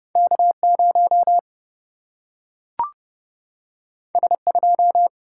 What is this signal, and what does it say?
Morse code, heard as a keyed tone, about 30 wpm, 700 Hz. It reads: K0 S2
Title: Single-Letter Single-Number 30wpm